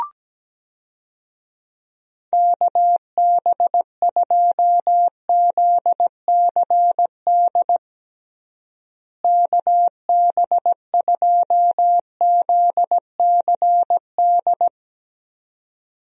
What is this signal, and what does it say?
Morse code, heard as a keyed tone, about 17 wpm, 700 Hz. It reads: KB2ZCD KB2ZCD
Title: USA Call Signs - 6 Characters - Repeated 2x - 17wpm